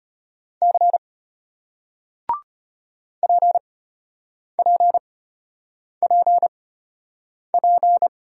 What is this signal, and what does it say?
Morse code, heard as a keyed tone, about 33 wpm, spaced as for 8 wpm, 700 Hz. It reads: C PPPP